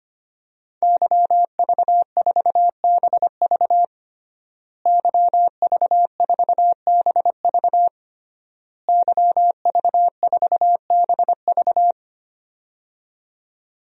Morse code, heard as a keyed tone, about 25 wpm, 700 Hz. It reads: YV4BV YV4BV YV4BV
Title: DXCC-100 Call Signs - 5 Characters - Repeated 3x - 25wpm